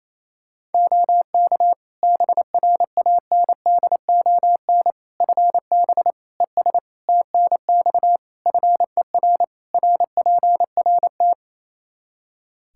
Morse code, heard as a keyed tone, about 28 wpm, 700 Hz. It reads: OK BRANDON FB ES TNX FER RPRT
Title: K1USN SST QSO Practice 28wpm